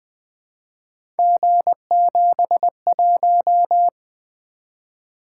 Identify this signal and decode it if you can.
Morse code, heard as a keyed tone, about 20 wpm, 700 Hz. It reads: G71
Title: Three Letter-Number Combinations - 20wpm